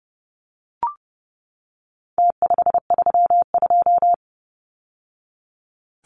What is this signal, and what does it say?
Morse code, heard as a keyed tone, about 30 wpm, 700 Hz. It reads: T532
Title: Numbers - Contest Style - 4-Digit 30wpm